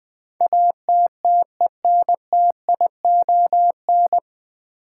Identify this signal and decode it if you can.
Morse code, heard as a keyed tone, about 20 wpm, 700 Hz. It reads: ATTENTION
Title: Sentences from Top 1000 Words - New Word Drill - 20wpm